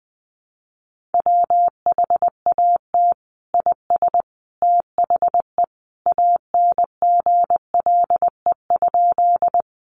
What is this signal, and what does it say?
Morse code, heard as a keyed tone, about 19 wpm, 700 Hz. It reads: WHAT IS THE ANGLE?
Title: Sentences from Top 1500 Words 20wpm